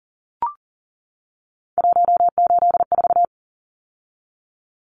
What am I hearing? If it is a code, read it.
Morse code, heard as a keyed tone, about 40 wpm, 700 Hz. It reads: 184
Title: Numbers - 3-Digit 40wpm